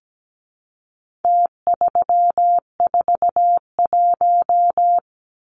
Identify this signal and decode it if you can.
Morse code, heard as a keyed tone, about 17 wpm, 700 Hz. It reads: T341
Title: Numbers - Contest Style - 4-Digit 17wpm